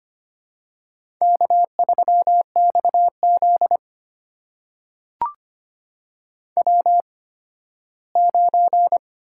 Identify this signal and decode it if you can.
Morse code, heard as a keyed tone, about 25 wpm, spaced as for 15 wpm, 700 Hz. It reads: K3XZ W9
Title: USA Call Signs - 4 Characters - 8x Character Spacing - 25wpm